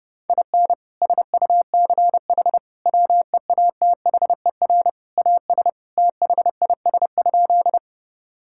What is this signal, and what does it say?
Morse code, heard as a keyed tone, about 30 wpm, 700 Hz. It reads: IN SUCH WEATHER AS THIS?
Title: Sentences from Top 1000 Words - Review Part 5 - 30wpm